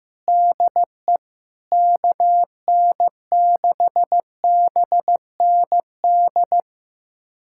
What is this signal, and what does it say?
Morse code, heard as a keyed tone, about 15 wpm, 700 Hz. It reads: DE KN6BND
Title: USA Callsign DE Callsign 15wpm